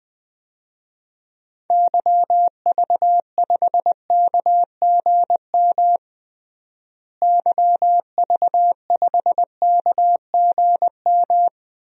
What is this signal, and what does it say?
Morse code, heard as a keyed tone, about 21 wpm, 700 Hz. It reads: YV5KGM YV5KGM
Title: DXCC-100 Call Signs - 6 Characters - Repeated 3x - 20wpm